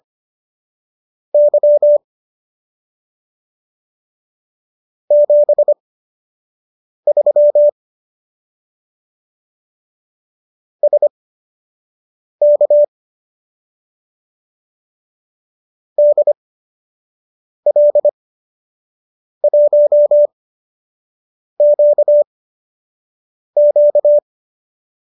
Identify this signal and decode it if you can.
Morse code, heard as a keyed tone, about 25 wpm, spaced as for 6 wpm, 590 Hz. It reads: Y 73 SK DL1QQ
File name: qso012 06